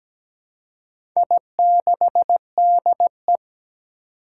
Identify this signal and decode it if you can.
Morse code, heard as a keyed tone, about 17 wpm, 700 Hz. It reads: I6DE
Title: Four Challenging Letter-Number Combinations 17wpm